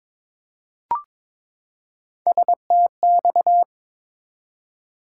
Morse code, heard as a keyed tone, about 22 wpm, 700 Hz. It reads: STX
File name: ARRL Sections v2 - 22wpm